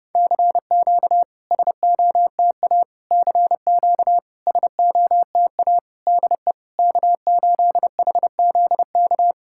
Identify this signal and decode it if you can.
Morse code, heard as a keyed tone, about 31 wpm, 690 Hz. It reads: CQ SOTA CQ SOTA DE K8HZK
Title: SOTA QSO Practice 30wpm